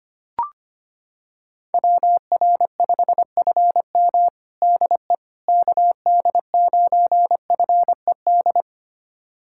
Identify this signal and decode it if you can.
Morse code, heard as a keyed tone, about 25 wpm, 700 Hz. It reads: WR5FM DE KD9FED
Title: USA Callsign DE Callsign 25wpm